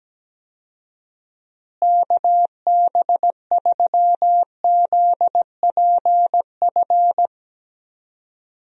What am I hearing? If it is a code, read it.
Morse code, heard as a keyed tone, about 17 wpm, 700 Hz. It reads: KB3ZPF
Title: USA Call Signs - 6 Characters - Repeated 3x - 17wpm